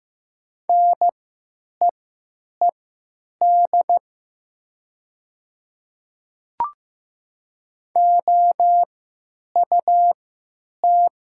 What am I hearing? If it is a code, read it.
Morse code, heard as a keyed tone, about 15 wpm, spaced as for 8 wpm, 700 Hz. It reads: NEED OUT
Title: 3x Character Spacing - Top 300 Words - 15wpm